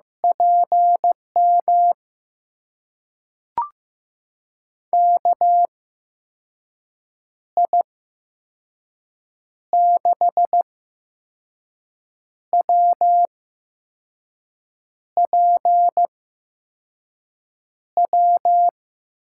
Morse code, heard as a keyed tone, about 15 wpm, spaced as for 5 wpm, 700 Hz. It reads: PM KI6WPW
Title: USA Call Signs - 6 Characters - 8x Character Spacing - 15wpm